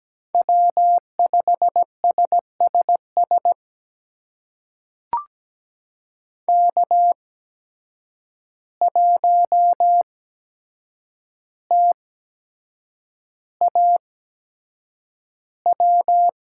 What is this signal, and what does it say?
Morse code, heard as a keyed tone, about 17 wpm, spaced as for 7 wpm, 700 Hz. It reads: W5SSS K1TAW
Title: USA Call Signs - 5 Characters - 8x Character Spacing - 17wpm